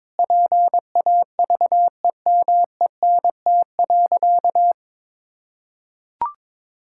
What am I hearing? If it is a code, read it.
Morse code, heard as a keyed tone, about 22 wpm, 700 Hz. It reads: PAVEMENT.
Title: Pangram Challenge v2 - 22wpm